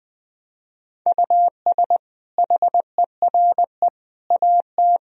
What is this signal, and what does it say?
Morse code, heard as a keyed tone, about 20 wpm, 700 Hz. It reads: US HERE AT
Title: Speed-Racing - Sets of 5 Words 20wpm